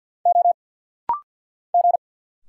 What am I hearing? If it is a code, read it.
Morse code, heard as a keyed tone, about 50 wpm, 700 Hz. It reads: O G
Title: Single Letters - Rapid-Fire 50wpm